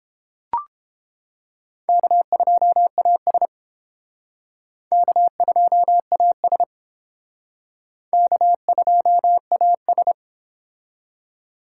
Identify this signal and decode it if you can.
Morse code, heard as a keyed tone, about 30 wpm, 700 Hz. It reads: K2AS K2AS K2AS